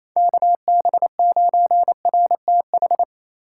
Morse code, heard as a keyed tone, about 35 wpm, 700 Hz. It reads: KB9RTH
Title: Speed Racing - USA 2x3 Call Signs 28wpm